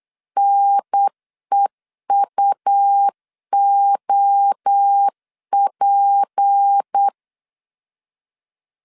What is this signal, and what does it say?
Morse code, heard as a keyed tone, about 8 wpm, 800 Hz. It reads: NEUOP
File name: CW / DET / 1 6 1500C S B 8wpm 5142